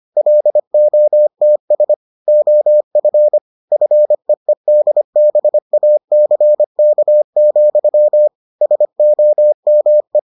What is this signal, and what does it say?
Morse code, heard as a keyed tone, about 25 wpm, 600 Hz. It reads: LOTS OF FEEDBACK, SOME